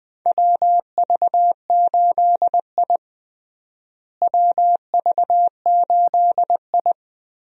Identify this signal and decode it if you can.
Morse code, heard as a keyed tone, about 20 wpm, 700 Hz. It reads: WV8I WV8I
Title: USA Call Signs - 4 Characters - Repeated 3x - 20wpm